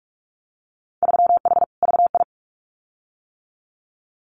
Morse code, heard as a keyed tone, about 44 wpm, 700 Hz. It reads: 3H VI
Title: Winter Field Day Exchanges Encore 45wpm